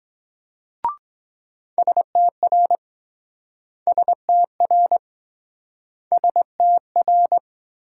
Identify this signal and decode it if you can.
Morse code, heard as a keyed tone, about 23 wpm, 700 Hz. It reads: STR STR STR